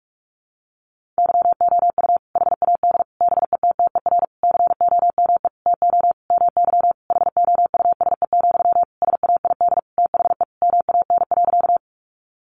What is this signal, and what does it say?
Morse code, heard as a keyed tone, about 45 wpm, 700 Hz. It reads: YOU HAD BETTER COME TO MY HOUSE, SAID THE MAN.
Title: Sentences from Top 400 Words 45wpm